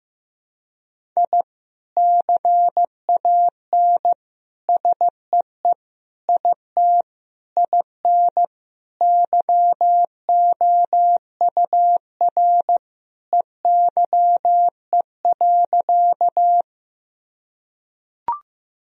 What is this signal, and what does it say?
Morse code, heard as a keyed tone, about 15 wpm, 700 Hz. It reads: I CAN SEE IT IN YOUR EYE.
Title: Sentences from Top 1000 Words - Review Part 1 - 15wpm